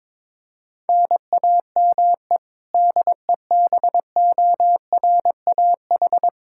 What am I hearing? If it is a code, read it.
Morse code, heard as a keyed tone, about 22 wpm, 700 Hz. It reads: NAME DEBORAH
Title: Practice QSO - Exchange 1 - Beginning - 22wpm